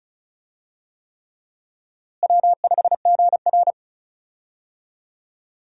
Morse code, heard as a keyed tone, about 36 wpm, 700 Hz. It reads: W5GR
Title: USA Call Signs - 4 Characters - Repeated 3x - 35wpm